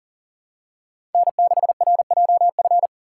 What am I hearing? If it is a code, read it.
Morse code, heard as a keyed tone, about 40 wpm, 700 Hz. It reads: N6RJF
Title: USA Call Signs - 6x Character Spacing - 40wpm